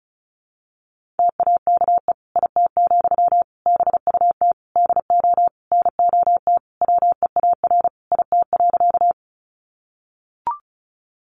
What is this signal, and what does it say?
Morse code, heard as a keyed tone, about 34 wpm, 700 Hz. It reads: TAKE IT, BUT DO NOT WEAR IT.
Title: Sentences from Top 1500 Words 35wpm